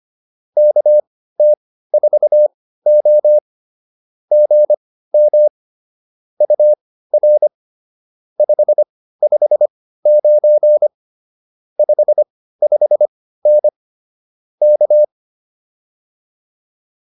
Morse code, heard as a keyed tone, about 25 wpm, spaced as for 15 wpm, 600 Hz. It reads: KT4O GM UR 559 55N K